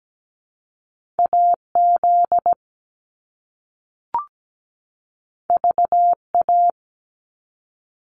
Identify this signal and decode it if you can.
Morse code, heard as a keyed tone, about 17 wpm, 700 Hz. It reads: AZ VA
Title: US States 17wpm